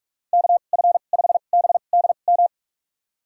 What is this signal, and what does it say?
Morse code, heard as a keyed tone, about 46 wpm, spaced as for 33 wpm, 700 Hz. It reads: KF5BDM
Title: USA Call Signs - 6 Characters - 2x Character Spacing - 45wpm